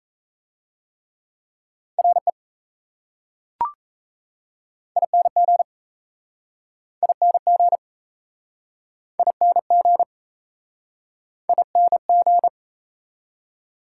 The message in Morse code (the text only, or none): RE ING ING ING ING